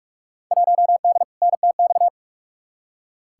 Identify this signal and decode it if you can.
Morse code, heard as a keyed tone, about 46 wpm, 700 Hz. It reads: 1D NTX
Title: Field Day Exchanges - 45wpm